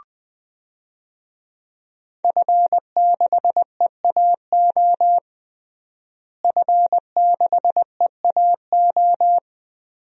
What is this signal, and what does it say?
Morse code, heard as a keyed tone, about 20 wpm, 700 Hz. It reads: F6EAO F6EAO
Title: DXCC-100 Call Signs - 5 Characters - Repeated 2x - 20wpm